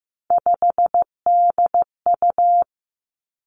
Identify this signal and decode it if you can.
Morse code, heard as a keyed tone, about 15 wpm, 700 Hz. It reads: 5DU
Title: Three Challenging Letter-Number Combinations 15wpm